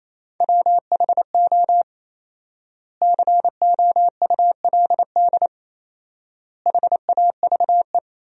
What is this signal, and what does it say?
Morse code, heard as a keyed tone, about 28 wpm, 700 Hz. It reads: WHO COULD HAVE
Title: 4x Word-Spacing - Sentences from Top 300 Words - 28wpm